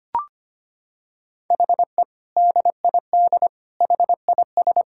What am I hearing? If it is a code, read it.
Morse code, heard as a keyed tone, about 25 wpm, 700 Hz. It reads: HE DID HIS
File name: practice / 25wpm / Sentences with 2-Word Phrases - 25wpm